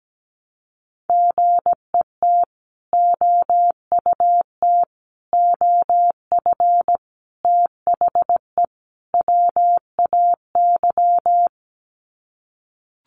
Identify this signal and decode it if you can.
Morse code, heard as a keyed tone, about 17 wpm, 700 Hz. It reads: GET OUT OF THE WAY
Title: Sets of 5 Words Encore 17wpm